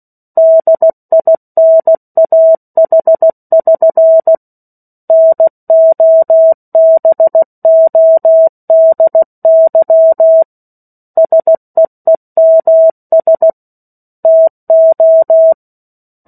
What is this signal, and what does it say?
Morse code, heard as a keyed tone, about 16 wpm, 650 Hz. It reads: DINAH<SN> NOBODY SEEMS TO
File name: 03-Chapter-03 16 WPM